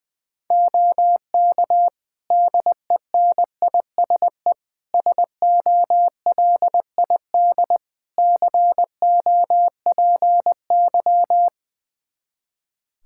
Morse code, heard as a keyed tone, about 20 wpm, 700 Hz. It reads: OK DENISE SOLID COPY
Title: Practice QSO - Exchange 3 - More Info - 20wpm